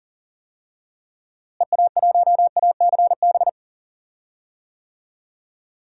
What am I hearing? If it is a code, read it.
Morse code, heard as a keyed tone, about 40 wpm, 700 Hz. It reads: EA1ACB
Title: DXCC-100 Call Signs - 6 Characters - Repeated 3x - 40wpm